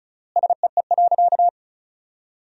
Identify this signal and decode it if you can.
Morse code, heard as a keyed tone, about 36 wpm, 700 Hz. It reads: SEE.
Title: 3x Word-Spacing - Sentences from Top 100 Words - 35wpm